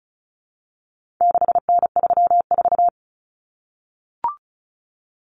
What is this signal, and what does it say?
Morse code, heard as a keyed tone, about 34 wpm, 700 Hz. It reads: 6N34
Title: Common Cut Numbers - 4-Digit 35wpm